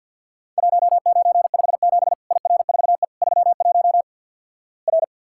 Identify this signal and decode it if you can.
Morse code, heard as a keyed tone, about 53 wpm, 700 Hz. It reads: 1957 IR4E 31 R
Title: Worked All Europe DX Contest - QTC Exchange - 50wpm